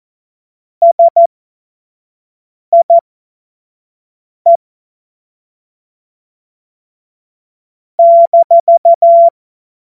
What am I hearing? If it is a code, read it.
Morse code, heard as a keyed tone, about 14 wpm, spaced as for 5 wpm, 680 Hz. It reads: SIE -
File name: Morsekurs / Witze / Witz 12. April 2022 25 F70